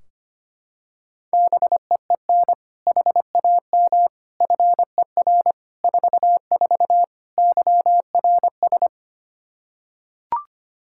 Morse code, heard as a keyed tone, about 25 wpm, 700 Hz. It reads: BEEN HAM FER 44 YRS
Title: Practice QSO - Exchange 3 - More Info - 25wpm